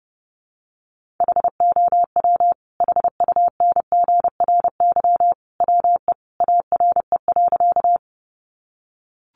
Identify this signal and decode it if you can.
Morse code, heard as a keyed tone, about 30 wpm, 700 Hz. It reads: HOW HUNGRY WE ARE.
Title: Sentences from Top 1400 Words 30wpm